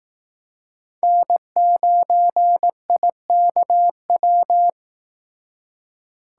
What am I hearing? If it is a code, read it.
Morse code, heard as a keyed tone, about 18 wpm, 700 Hz. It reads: N9IKW
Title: Speed Racing - USA 1x3 Call Signs 15wpm